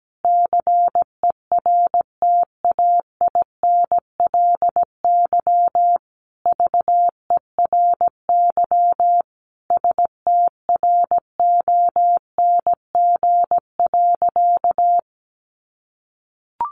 Morse code, heard as a keyed tone, about 17 wpm, 700 Hz. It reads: CERTAINLY VERY STRONG.
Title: Sentences from Top 1300 Words 17wpm